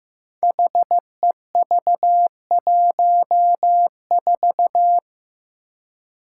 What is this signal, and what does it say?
Morse code, heard as a keyed tone, about 15 wpm, 700 Hz. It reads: HEV14
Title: Five Challenging Letter-Number Combinations 15wpm